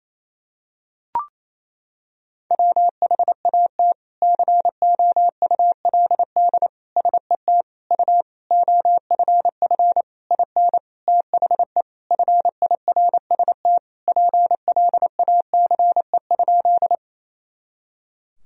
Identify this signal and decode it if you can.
Morse code, heard as a keyed tone, about 28 wpm, 700 Hz. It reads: WHAT COULD SET U OFF IN THE FIRST PLACE?
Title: Sentences from Top 200 Words and Abbreviations 28wpm